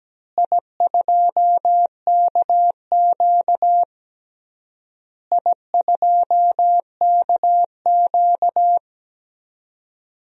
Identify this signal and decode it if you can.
Morse code, heard as a keyed tone, about 17 wpm, 700 Hz. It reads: I2KQ I2KQ
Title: DXCC-100 Call Signs - 4 Characters - Repeated 3x - 17wpm